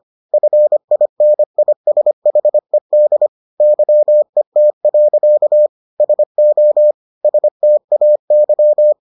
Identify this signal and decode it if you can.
Morse code, heard as a keyed tone, about 25 wpm, 600 Hz. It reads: FINISHED YET. SO STAY